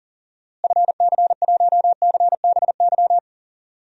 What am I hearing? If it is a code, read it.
Morse code, heard as a keyed tone, about 40 wpm, 700 Hz. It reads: FC1CBY
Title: DXCC-100 Call Signs - 6 Characters - Repeated 2x - 40wpm